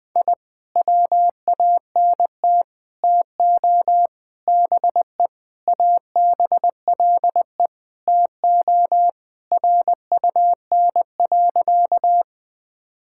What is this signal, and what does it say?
Morse code, heard as a keyed tone, about 20 wpm, 700 Hz. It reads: I WANT TO BE ABLE TO RUN.
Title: Sentences with 3-Word Phrases - 20wpm